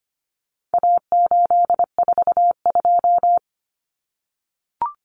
Morse code, heard as a keyed tone, about 25 wpm, 700 Hz. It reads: A842
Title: Common Cut Numbers - 4-Digit 25wpm